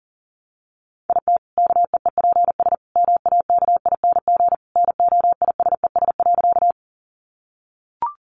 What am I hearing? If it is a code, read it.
Morse code, heard as a keyed tone, about 40 wpm, 700 Hz. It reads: IT KEEPS MAKING NOISES.
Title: Additional Easy Phrases 40wpm